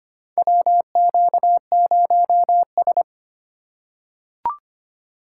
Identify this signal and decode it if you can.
Morse code, heard as a keyed tone, about 25 wpm, 700 Hz. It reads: WQ0S
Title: USA Call Signs - 4 Characters - 2x Character Spacing - 25wpm